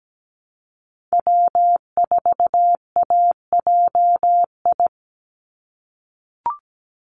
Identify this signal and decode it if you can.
Morse code, heard as a keyed tone, about 17 wpm, 700 Hz. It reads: W4AJI
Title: USA Call Signs 17wpm